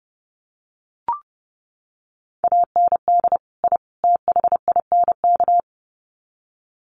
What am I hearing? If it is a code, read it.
Morse code, heard as a keyed tone, about 30 wpm, 700 Hz. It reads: AND I THINK
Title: Sets of 3 Words 30wpm